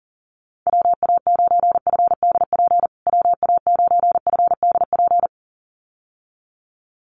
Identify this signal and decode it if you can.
Morse code, heard as a keyed tone, about 40 wpm, 700 Hz. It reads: WA9FDP WA9FDP
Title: USA Call Signs Repeated 2x 40wpm